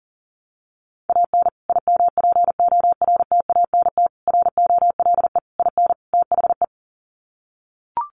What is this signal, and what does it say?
Morse code, heard as a keyed tone, about 40 wpm, 700 Hz. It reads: AN IMPORTANT ROLE IN THE
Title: Sets of 5 Words Encore2 40wpm